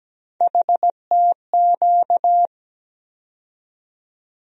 Morse code, heard as a keyed tone, about 17 wpm, 700 Hz. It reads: HTQ
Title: USA Call Sign Postfixes - 3 Letters 17wpm